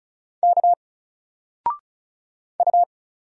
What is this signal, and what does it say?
Morse code, heard as a keyed tone, about 36 wpm, 700 Hz. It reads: K U
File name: Single-Letter-Number-35wpm